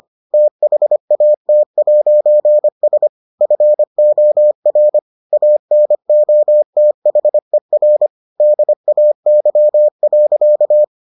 Morse code, heard as a keyed tone, about 25 wpm, 600 Hz. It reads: THAT'S FOR ANOTHER DAY.